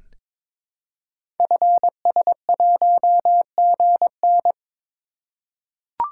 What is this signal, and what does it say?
Morse code, heard as a keyed tone, about 22 wpm, 700 Hz. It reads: FS1GN
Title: Code Groups - 5 Letter:Numbers - 22wpm